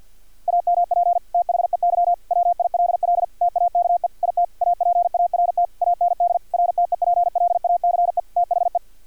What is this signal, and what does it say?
Morse code, heard as a keyed tone, about 50 wpm, 700 Hz, with radio noise. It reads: AMP THEY WILL TAKE IT APART AND RTEPLACE THE
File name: archived / CRSnet